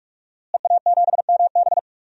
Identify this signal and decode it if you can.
Morse code, heard as a keyed tone, about 46 wpm, 700 Hz. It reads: EA7MB
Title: DXCC-100 Call Signs - 5 Characters - Repeated 2x - 45wpm